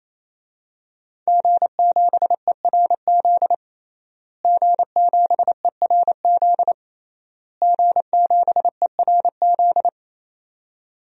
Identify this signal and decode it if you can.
Morse code, heard as a keyed tone, about 28 wpm, 700 Hz. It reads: G7ERZ G7ERZ G7ERZ
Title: DXCC-100 Call Signs - 5 Characters - Repeated 3x - 28wpm